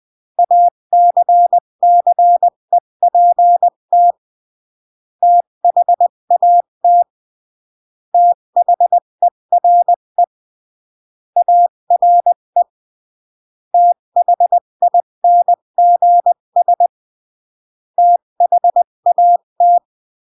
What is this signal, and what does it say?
Morse code, heard as a keyed tone, about 20 wpm, 700 Hz. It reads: ACCEPT THAT THERE ARE THINGS THAT
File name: CW / jodorowsky / 82 commandments 20-15